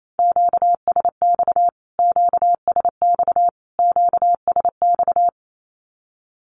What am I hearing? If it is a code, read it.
Morse code, heard as a keyed tone, about 28 wpm, 700 Hz. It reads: QSX QSX QSX
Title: Q-Codes Repeated 3x 28wpm